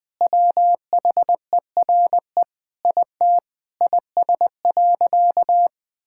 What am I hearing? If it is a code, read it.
Morse code, heard as a keyed tone, about 21 wpm, 700 Hz. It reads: WHERE IT IS.
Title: Sentences with 4-Word Phrases - 20wpm